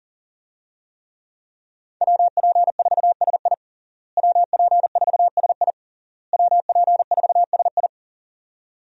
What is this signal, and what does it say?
Morse code, heard as a keyed tone, about 37 wpm, 700 Hz. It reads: WP4SI WP4SI WP4SI
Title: USA Call Signs - 5 Characters - Repeated 3x - 40wpm